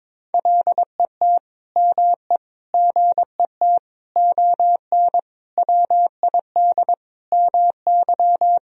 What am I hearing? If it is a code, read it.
Morse code, heard as a keyed tone, about 22 wpm, 700 Hz. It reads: LET ME GET ON WID MY
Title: Sentences from Top 300 Words and Abbreviations 22wpm